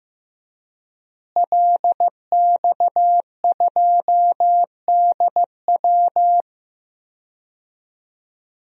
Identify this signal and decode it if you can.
Morse code, heard as a keyed tone, about 15 wpm, 700 Hz. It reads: LX2DW
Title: DXCC-100 Call Signs - 5 Characters - Repeated 2x - 15wpm